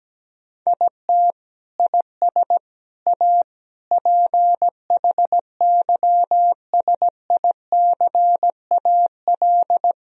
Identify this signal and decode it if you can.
Morse code, heard as a keyed tone, about 17 wpm, 700 Hz. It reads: IT IS A PHYSICAL
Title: Sentences from Top 1000 Words - Review Part 4 - 17wpm